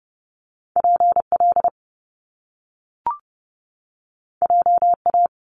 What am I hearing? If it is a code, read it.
Morse code, heard as a keyed tone, about 30 wpm, 700 Hz. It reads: PL JA
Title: Top Two Letter Combinations in Words 30wpm